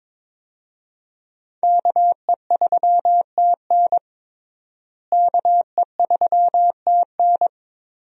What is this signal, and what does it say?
Morse code, heard as a keyed tone, about 22 wpm, 700 Hz. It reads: KE3TN KE3TN
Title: USA Call Signs - 5 Characters - Repeated 3x - 22wpm